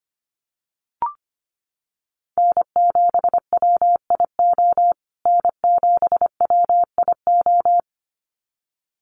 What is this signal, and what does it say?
Morse code, heard as a keyed tone, about 25 wpm, 700 Hz. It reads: N7WIO N7WIO
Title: USA Call Signs Repeated 2x 25wpm